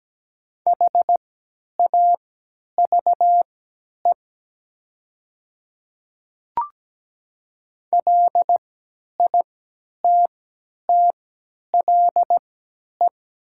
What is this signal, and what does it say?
Morse code, heard as a keyed tone, about 17 wpm, spaced as for 10 wpm, 700 Hz. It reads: HAVE LITTLE
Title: 3x Character Spacing - Top 300 Words - 17wpm